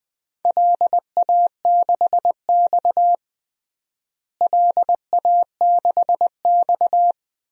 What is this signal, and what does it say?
Morse code, heard as a keyed tone, about 20 wpm, 700 Hz. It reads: LA6X LA6X
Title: DXCC-100 Call Signs - 4 Characters - Repeated 3x - 20wpm